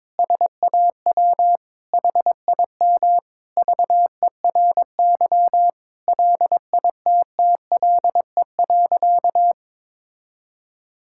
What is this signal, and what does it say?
Morse code, heard as a keyed tone, about 21 wpm, 700 Hz. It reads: SAW HIM VERY LITTLE.
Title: Sentences from Top 1000 Words - Review Part 5 - 22wpm